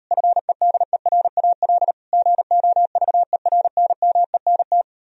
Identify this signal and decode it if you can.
Morse code, heard as a keyed tone, about 38 wpm, 700 Hz. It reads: FEDERAL GOVERNMENT